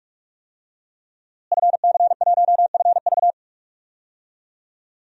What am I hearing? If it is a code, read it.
Morse code, heard as a keyed tone, about 44 wpm, 700 Hz. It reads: FC1FV
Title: DXCC-100 Call Signs - 5 Characters - Repeated 3x - 45wpm